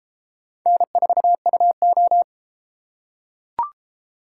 Morse code, heard as a keyed tone, about 29 wpm, 700 Hz. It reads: N4UO